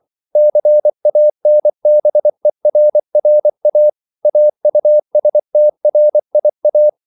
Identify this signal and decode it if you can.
Morse code, heard as a keyed tone, about 24 wpm, 600 Hz. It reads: CANBERRA AUSTRIA
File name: CW / Practice / NationalCapitals24